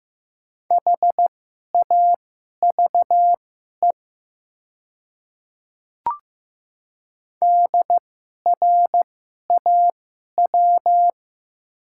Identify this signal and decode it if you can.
Morse code, heard as a keyed tone, about 15 wpm, spaced as for 11 wpm, 700 Hz. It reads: HAVE DRAW
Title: 2x Character Spacing - Top 300 Words - 15wpm